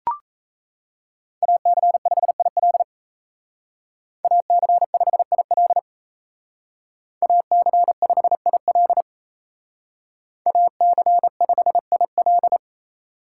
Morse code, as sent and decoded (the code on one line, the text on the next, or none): .- -.-. ..... .. .-.. / .- -.-. ..... .. .-.. / .- -.-. ..... .. .-.. / .- -.-. ..... .. .-..
AC5IL AC5IL AC5IL AC5IL